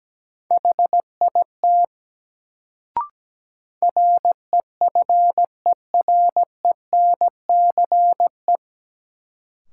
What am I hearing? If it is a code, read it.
Morse code, heard as a keyed tone, about 17 wpm, 700 Hz. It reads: HIT REFERENCE
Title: Sentences from Top 1200 Words - New Word Drill Alt - 17wpm